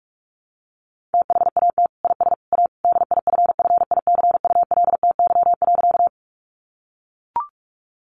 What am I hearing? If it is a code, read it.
Morse code, heard as a keyed tone, about 44 wpm, 700 Hz. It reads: THAT IS A DIFFICULTY.
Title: Sentences from Top 1500 Words 45wpm